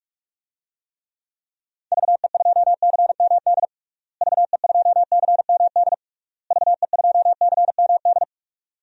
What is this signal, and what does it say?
Morse code, heard as a keyed tone, about 46 wpm, 700 Hz. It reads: VE2CMD VE2CMD VE2CMD
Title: DXCC-100 Call Signs - 6 Characters - Repeated 3x - 45wpm